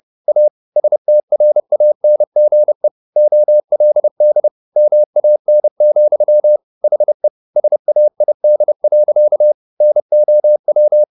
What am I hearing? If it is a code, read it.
Morse code, heard as a keyed tone, about 30 wpm, 600 Hz. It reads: A STRANGE OLD MAN, HE SAID. NOW
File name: the old man and the sea / the old man and the sea 1 30wpm